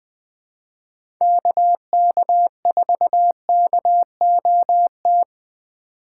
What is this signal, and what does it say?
Morse code, heard as a keyed tone, about 20 wpm, 700 Hz. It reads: KK4KOT
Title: USA Call Signs - 6 Characters - 6x Character Spacing - 20wpm